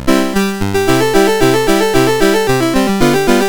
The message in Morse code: none